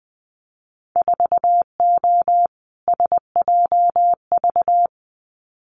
Morse code, heard as a keyed tone, about 20 wpm, 700 Hz. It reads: 4O SJV
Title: Winter Field Day Exchanges Encore 20wpm